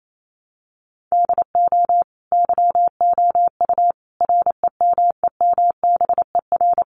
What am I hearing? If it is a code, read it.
Morse code, heard as a keyed tone, about 28 wpm, 700 Hz. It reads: DO YOU REMEMBER
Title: Sentences from Top 400 Words 28wpm